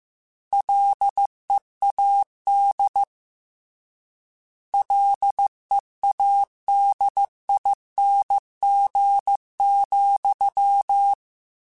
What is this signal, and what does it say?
Morse code, heard as a keyed tone, about 15 wpm, 800 Hz. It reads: LEAD LEADING,
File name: Combinations / Combinations-mp3s / Combinations@13